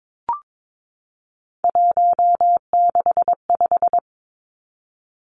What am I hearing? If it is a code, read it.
Morse code, heard as a keyed tone, about 22 wpm, 700 Hz. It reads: 165
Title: Three Challenging Letter-Number Combinations 22wpm